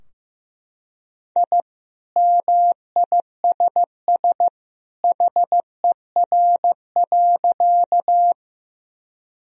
Morse code, heard as a keyed tone, about 15 wpm, 700 Hz. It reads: I MISS HER.
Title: Sentences from Top 1000 Words - Review Part 3 - 15wpm